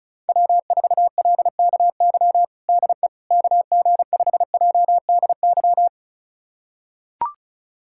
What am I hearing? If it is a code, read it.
Morse code, heard as a keyed tone, about 36 wpm, 700 Hz. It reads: W4LKY DE KG5JDY
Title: USA Callsign DE Callsign 35wpm